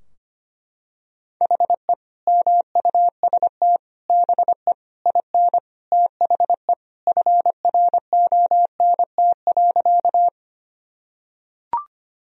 Morse code, heard as a keyed tone, about 25 wpm, 700 Hz. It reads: HE MUST BE IN THE FRONT.
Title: Sentences from Top 1000 Words - Review Part 2 - 25wpm